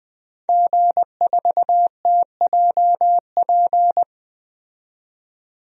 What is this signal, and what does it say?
Morse code, heard as a keyed tone, about 20 wpm, 700 Hz. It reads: G4TJP
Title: DXCC-100 Call Signs - 5 Characters - 20wpm